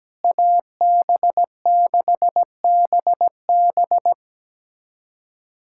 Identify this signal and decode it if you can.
Morse code, heard as a keyed tone, about 17 wpm, 700 Hz. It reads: AB6BB
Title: USA Callsign DE Callsign - 5 Characters - 17wpm